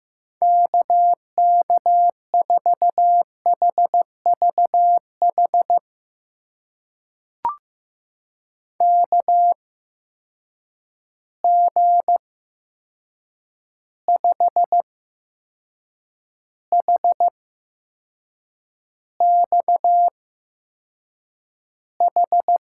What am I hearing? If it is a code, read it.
Morse code, heard as a keyed tone, about 15 wpm, spaced as for 6 wpm, 700 Hz. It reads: KK4HVH KG5HXH
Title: USA Call Signs - 6 Characters - 8x Character Spacing - 15wpm